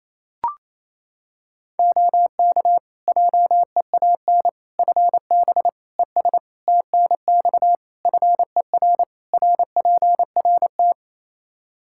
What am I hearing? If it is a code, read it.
Morse code, heard as a keyed tone, about 28 wpm, 700 Hz. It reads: OK JEAN FB ES TNX FER RPRT
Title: Practice QSO - Exchange 2 - Info - 28wpm